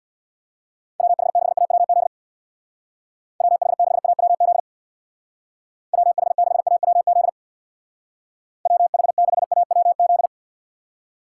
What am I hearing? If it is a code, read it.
Morse code, heard as a keyed tone, about 61 wpm, 700 Hz. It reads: WH6AWZ WH6AWZ WH6AWZ WH6AWZ
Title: Speed Racing - USA 2x3 Call Signs 50wpm